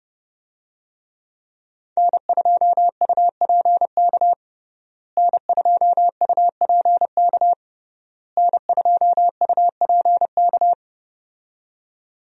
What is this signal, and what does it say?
Morse code, heard as a keyed tone, about 30 wpm, 700 Hz. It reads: N2UPK N2UPK N2UPK
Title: USA Call Signs - 5 Characters - Repeated 3x - 30wpm